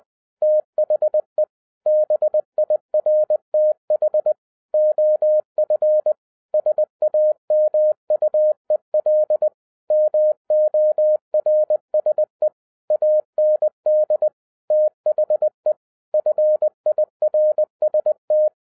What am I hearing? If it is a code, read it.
Morse code, heard as a keyed tone, about 20 wpm, 610 Hz. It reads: THE BIRTH OF SAMUEL MORSE AND THE FIRST